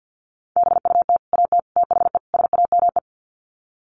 Morse code, heard as a keyed tone, about 50 wpm, 700 Hz. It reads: BUT AT THE SAME
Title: Sets of 4 Words 50wpm